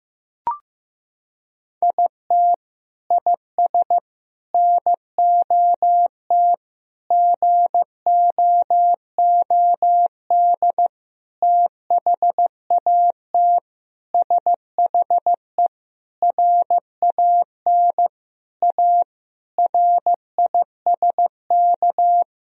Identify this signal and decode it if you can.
Morse code, heard as a keyed tone, about 15 wpm, 700 Hz. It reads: IT IS NOT GOOD THAT SHE RAN A RISK
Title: Sentences from Top 1000 Words - Review Part 4 - 15wpm